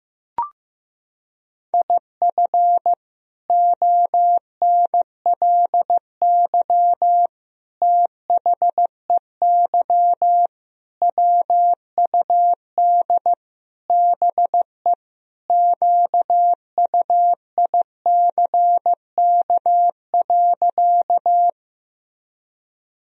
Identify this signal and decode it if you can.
Morse code, heard as a keyed tone, about 15 wpm, 700 Hz. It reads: IF ONLY THEY WUD BE QUICK.
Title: Sentences from Top 500 Words and Abbreviations 15wpm